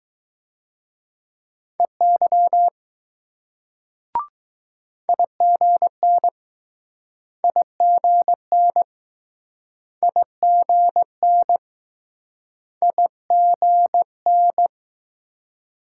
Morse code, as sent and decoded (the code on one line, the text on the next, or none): . -.-- / .. --. -. / .. --. -. / .. --. -. / .. --. -.
EY IGN IGN IGN IGN